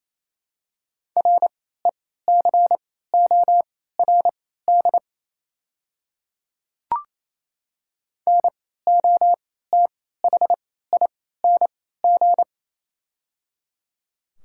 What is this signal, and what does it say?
Morse code, heard as a keyed tone, about 28 wpm, spaced as for 16 wpm, 700 Hz. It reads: RECORD NOTHING
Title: 3x Character Spacing - Top 500 Words - 28wpm